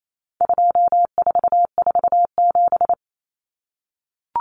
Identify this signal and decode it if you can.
Morse code, heard as a keyed tone, about 28 wpm, 700 Hz. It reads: 2447
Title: Numbers - 4-Digit 28wpm